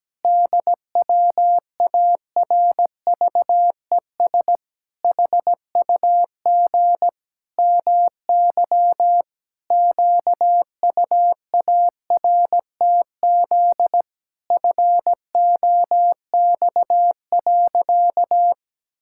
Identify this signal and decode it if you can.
Morse code, heard as a keyed tone, about 17 wpm, 700 Hz. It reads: DWARVES HUG MY QUARTZ FOX.
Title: Pangram Challenge v2 - 17wpm